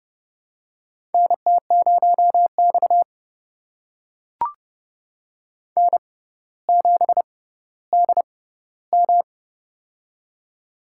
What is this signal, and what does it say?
Morse code, heard as a keyed tone, about 30 wpm, spaced as for 15 wpm, 700 Hz. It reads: NT0X N7DM
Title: USA Call Signs - 6x Character Spacing - 30wpm